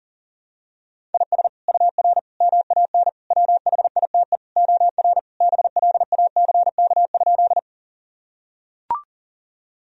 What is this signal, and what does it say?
Morse code, heard as a keyed tone, about 40 wpm, 700 Hz. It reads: IS UR MAN WHITE OR BLACK?
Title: Sentences from Top 400 Words and Abbreviations 40wpm